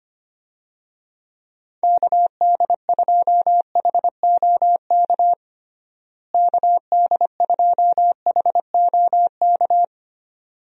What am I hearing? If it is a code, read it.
Morse code, heard as a keyed tone, about 25 wpm, 700 Hz. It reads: KD2HOK KD2HOK
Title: USA Call Signs - 6 Characters - Repeated 2x - 25wpm